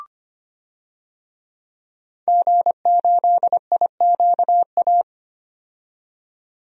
Morse code, heard as a keyed tone, about 25 wpm, 700 Hz. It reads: G8IQA
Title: DXCC-100 Call Signs - 5 Characters - 25wpm